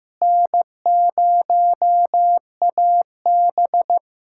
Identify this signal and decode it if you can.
Morse code, heard as a keyed tone, about 15 wpm, 700 Hz. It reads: N0AB
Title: USA Callsign DE Callsign - 4 Characters - 15wpm